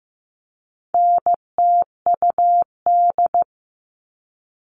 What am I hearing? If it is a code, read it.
Morse code, heard as a keyed tone, about 15 wpm, 700 Hz. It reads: NTUD
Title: All Cut Numbers - 4-Digit 15wpm